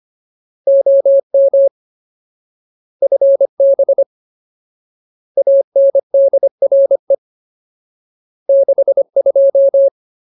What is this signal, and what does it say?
Morse code, heard as a keyed tone, about 25 wpm, 550 Hz. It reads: OM FB ANDRE 62
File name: Ultimate Word List / 25WPM Ultimate